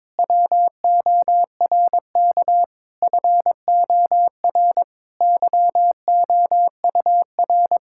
Morse code, heard as a keyed tone, about 22 wpm, 700 Hz. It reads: WORK FOR YOUR
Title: Sentences from Top 1000 Words - Review Part 6 - 22wpm